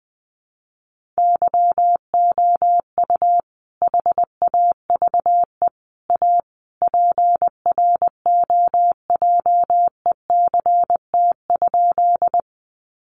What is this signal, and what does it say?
Morse code, heard as a keyed tone, about 20 wpm, 700 Hz. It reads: YOU HAVE A PROJECT?
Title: Sentences from Top 1400 Words 20wpm